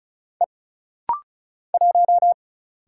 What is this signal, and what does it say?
Morse code, heard as a keyed tone, about 35 wpm, 700 Hz. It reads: E 1
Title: Single Letter-Number Alt - Rapid-Fire 35wpm